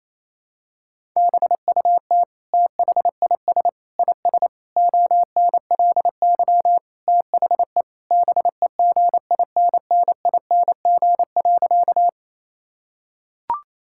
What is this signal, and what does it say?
Morse code, heard as a keyed tone, about 28 wpm, 700 Hz. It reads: BUT THIS IS ONLY THE BEGINNING.
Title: Sentences from Top 1000 Words - Review Part 5 - 28wpm